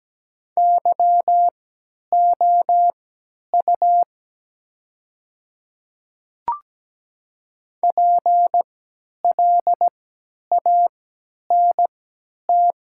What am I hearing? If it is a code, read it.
Morse code, heard as a keyed tone, about 17 wpm, spaced as for 10 wpm, 700 Hz. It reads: YOU PLANT
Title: 3x Character Spacing - Top 300 Words - 17wpm